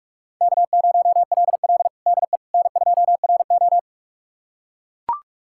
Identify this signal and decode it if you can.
Morse code, heard as a keyed tone, about 44 wpm, 700 Hz. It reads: K0LL DE N2RO
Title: USA Callsign DE Callsign - 4 Characters - 45wpm